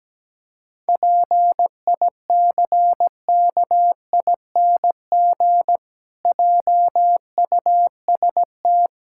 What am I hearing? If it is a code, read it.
Morse code, heard as a keyed tone, about 17 wpm, 700 Hz. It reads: PICKING JUST
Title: Pangram Challenge v2 - 17wpm